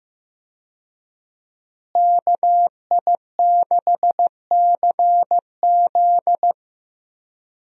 Morse code, heard as a keyed tone, about 16 wpm, 700 Hz. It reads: KI6CZ
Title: USA Call Signs - 5 Characters - Repeated 2x - 15wpm